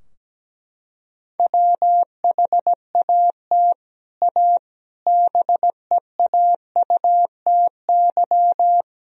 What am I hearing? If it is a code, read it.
Morse code, heard as a keyed tone, about 17 wpm, 700 Hz. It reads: WHAT A BEAUTY
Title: Sentences from Top 1000 Words - Review Part 2 - 17wpm